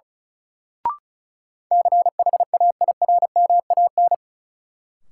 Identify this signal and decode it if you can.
Morse code, heard as a keyed tone, about 36 wpm, 700 Hz. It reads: CHAIRMAN
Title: Sets of 3 Words - New Word Drill - 35wpm